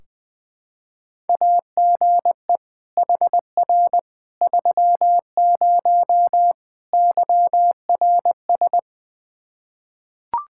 Morse code, heard as a keyed tone, about 20 wpm, 700 Hz. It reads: AGE HR 30 YRS
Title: Practice QSO - Exchange 3 - More Info - 20wpm